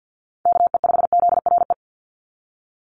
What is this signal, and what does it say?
Morse code, heard as a keyed tone, about 48 wpm, 700 Hz. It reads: KE5ZRE
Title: USA Call Signs 50wpm